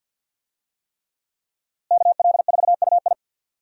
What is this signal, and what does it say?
Morse code, heard as a keyed tone, about 50 wpm, 700 Hz. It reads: KL4UI
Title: USA Call Signs - 5 Characters - Repeated 3x - 50wpm